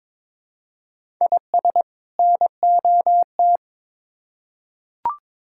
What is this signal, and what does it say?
Morse code, heard as a keyed tone, about 22 wpm, 700 Hz. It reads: IS NOT
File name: Build A Phrase - 2-Word Phrases - 22wpm